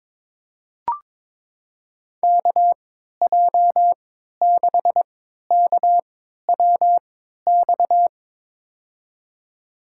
Morse code, heard as a keyed tone, about 22 wpm, spaced as for 13 wpm, 700 Hz. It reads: KJ6KWX
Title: USA Call Signs - 6 Characters - 3x Character Spacing - 22wpm